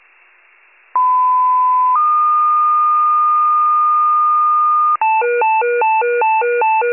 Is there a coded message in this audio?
Yes: two-tone page from 1.0 to 5.0 s, heard as a one-second tone, then a three-second tone, about 1010 then 1220 Hz.